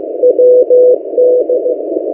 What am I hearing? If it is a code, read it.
Morse code, heard as a keyed tone, about 15 wpm, 500 Hz. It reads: WD
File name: M08a- -1355utc-8096khz